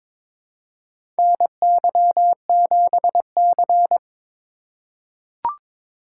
Morse code, heard as a keyed tone, about 22 wpm, 700 Hz. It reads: NY7C
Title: USA Call Signs - 4 Characters - 4x Character Spacing - 22wpm